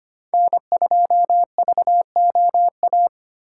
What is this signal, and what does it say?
Morse code, heard as a keyed tone, about 32 wpm, 700 Hz. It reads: N2VOA